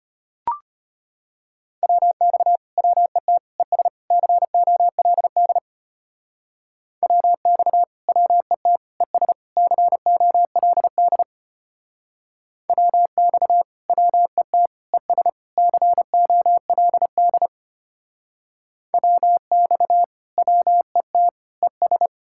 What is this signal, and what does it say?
Morse code, heard as a keyed tone, about 32 wpm, 700 Hz. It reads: WX WET ES COLD WX WET ES COLD WX WET ES COLD WX WET ES